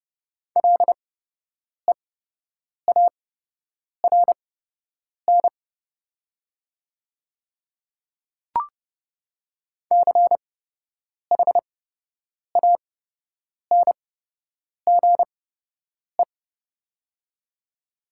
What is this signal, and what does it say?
Morse code, heard as a keyed tone, about 30 wpm, spaced as for 8 wpm, 700 Hz. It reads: LEARN CHANGE
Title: 8x Character Spacing - Top 500 Words - 30wpm